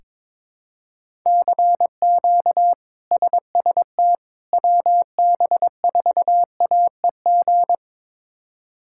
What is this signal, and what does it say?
Morse code, heard as a keyed tone, about 22 wpm, 700 Hz. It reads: CQ SST WB4AEG
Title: K1USN SST QSO Practice - 22wpm